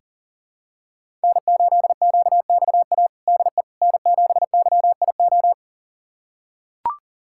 Morse code, heard as a keyed tone, about 40 wpm, 700 Hz. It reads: N8QXA DE N7YIO
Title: USA Callsign DE Callsign - 5 Characters - 40wpm